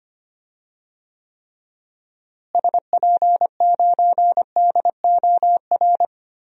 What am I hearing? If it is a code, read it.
Morse code, heard as a keyed tone, about 25 wpm, 700 Hz. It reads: SP9DOR
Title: DXCC-100 Call Signs - 6 Characters - Repeated 3x - 25wpm